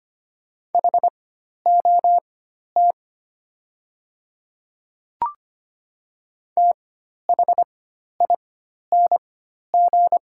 Morse code, heard as a keyed tone, about 25 wpm, spaced as for 12 wpm, 700 Hz. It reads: HOT THING
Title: 4x Character Spacing - Top 100 Words - 25wpm